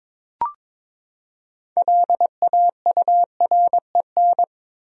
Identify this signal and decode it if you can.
Morse code, heard as a keyed tone, about 21 wpm, 700 Hz. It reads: LAUREN
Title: USA Common Names - 22wpm